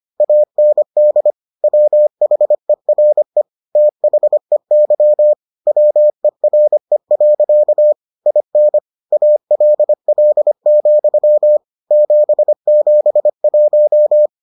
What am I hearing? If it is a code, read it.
Morse code, heard as a keyed tone, about 25 wpm, 600 Hz. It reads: AND WHERE THEY WERE. IN ALL, 771